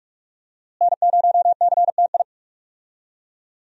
Morse code, heard as a keyed tone, about 47 wpm, 700 Hz. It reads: N0CTI
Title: USA Call Signs - 5 Characters - Repeated 3x - 45wpm